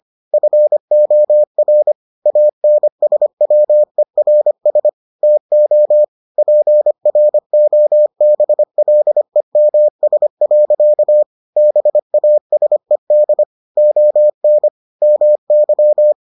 Morse code, heard as a keyed tone, about 25 wpm, 600 Hz. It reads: FOR ANSWERS TO PROBLEMS. BASED ON MY